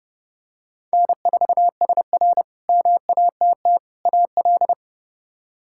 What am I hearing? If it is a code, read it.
Morse code, heard as a keyed tone, about 29 wpm, 700 Hz. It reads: N4SR MATT AL
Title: K1USN SST QSO Practice - 30wpm